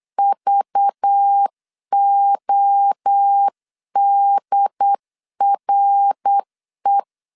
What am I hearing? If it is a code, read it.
Morse code, heard as a keyed tone, about 9 wpm, 800 Hz. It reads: VODRE
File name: CW / DET / 1 6 1500C S C 8wpm 5219